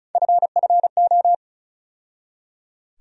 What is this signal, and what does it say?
Morse code, heard as a keyed tone, about 34 wpm, 700 Hz. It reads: FFO
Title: USA Call Sign Postfixes - 3 Letters 35wpm